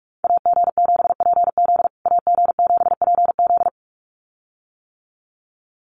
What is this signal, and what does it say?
Morse code, heard as a keyed tone, about 44 wpm, 690 Hz. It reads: AG7PZ AG7PZ
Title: USA Call Signs Repeated 2x 45wpm